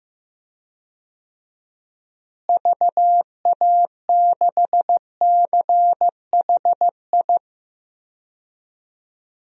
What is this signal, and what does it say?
Morse code, heard as a keyed tone, about 15 wpm, 700 Hz. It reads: VA6CHI
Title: DXCC-100 Call Signs - 6 Characters - Repeated 2x - 15wpm